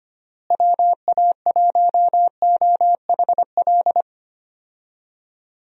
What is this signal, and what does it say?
Morse code, heard as a keyed tone, about 25 wpm, 700 Hz. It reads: WA1OHL
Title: USA Call Signs - 6 Characters - Repeated 2x - 25wpm